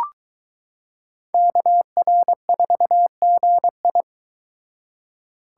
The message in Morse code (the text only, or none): KR4GI